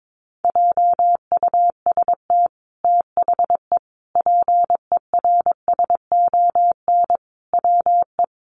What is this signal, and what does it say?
Morse code, heard as a keyed tone, about 22 wpm, 700 Hz. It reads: JUST THE PERSON WE
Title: Sentences from Top 900 Words 22wpm